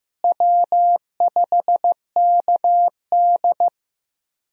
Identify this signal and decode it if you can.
Morse code, heard as a keyed tone, about 15 wpm, 700 Hz. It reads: W5KD
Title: USA Call Signs - 4 Characters - Repeated 3x - 15wpm